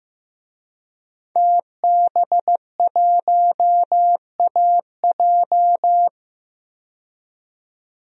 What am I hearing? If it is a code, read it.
Morse code, heard as a keyed tone, about 15 wpm, 700 Hz. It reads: TB1AJ
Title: DXCC-100 Call Signs - 5 Characters - 15wpm